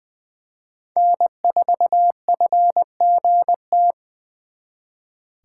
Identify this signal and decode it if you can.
Morse code, heard as a keyed tone, about 24 wpm, 700 Hz. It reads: N4FGT
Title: Speed Racing - USA 1x3 Call Signs 20wpm